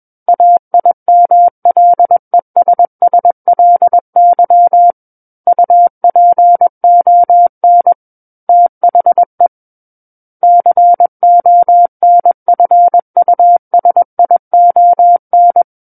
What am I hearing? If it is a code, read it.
Morse code, heard as a keyed tone, about 21 wpm, 700 Hz. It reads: AIMLESSLY UPON THE CONFUSION
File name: Book One / 12-Chapter 12 - 21 WPM